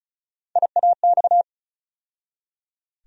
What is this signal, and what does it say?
Morse code, heard as a keyed tone, about 36 wpm, 700 Hz. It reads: IAX
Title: USA Call Sign Postfixes - 3 Letters 35wpm